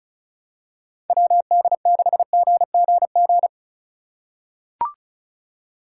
Morse code, heard as a keyed tone, about 36 wpm, 700 Hz. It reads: WD6GGG
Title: USA Call Signs - 6 Characters - 4x Character Spacing - 35wpm